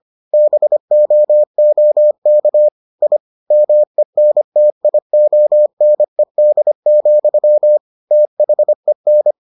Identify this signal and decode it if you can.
Morse code, heard as a keyed tone, about 25 wpm, 600 Hz. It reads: BOOK I MENTIONED, THEN